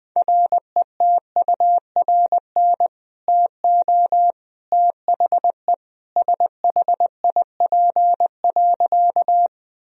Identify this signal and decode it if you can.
Morse code, heard as a keyed tone, about 21 wpm, 700 Hz. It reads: RETURN TO THE SHIP.
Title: Sentences from Top 1000 Words - Review Part 2 - 20wpm